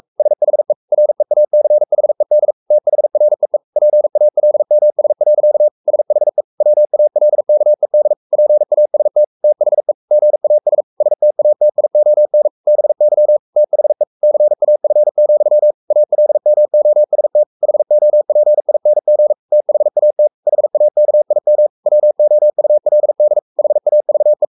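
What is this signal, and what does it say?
Morse code, heard as a keyed tone, about 44 wpm, 600 Hz. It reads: SHE REACHED THREE PALMS. SHE WALKED PAST THE GAS STATION BY THE CAF, ALMOST HOPING THAT HAKIM WOULD HAVE
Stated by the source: Morse code at 43 wpm sending Ken Follett's novel Never in English, from CW med Gnister.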